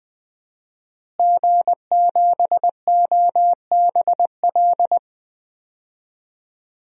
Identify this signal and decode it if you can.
Morse code, heard as a keyed tone, about 20 wpm, 700 Hz. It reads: G7OBL
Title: DXCC-100 Call Signs - 5 Characters - Repeated 3x - 20wpm